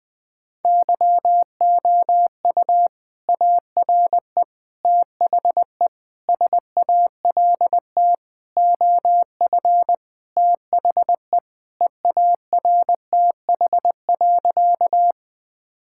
Morse code, heard as a keyed tone, about 21 wpm, 700 Hz. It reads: YOU ARE THE SALT OF THE EARTH.
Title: Sentences from Top 1000 Words - Review Part 5 - 20wpm